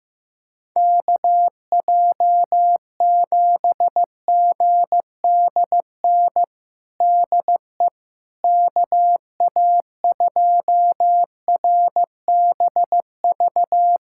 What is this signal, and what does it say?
Morse code, heard as a keyed tone, about 15 wpm, 700 Hz. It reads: KJ7GDN DE KA2RBV
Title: USA Callsign DE Callsign - 6 Characters - 15wpm